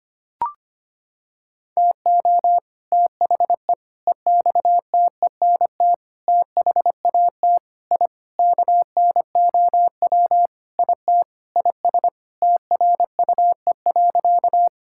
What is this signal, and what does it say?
Morse code, heard as a keyed tone, about 25 wpm, 700 Hz. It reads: TO THE EXTENT THAT I KNOW IT IS TRUE.
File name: practice / 25wpm / Sentences with 4-Word Phrases - 25wpm